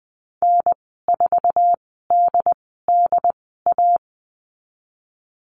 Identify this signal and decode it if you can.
Morse code, heard as a keyed tone, about 19 wpm, spaced as for 14 wpm, 700 Hz. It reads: N4DDA
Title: 161 Lesson 26g - Letter g - Call Signs 20wpm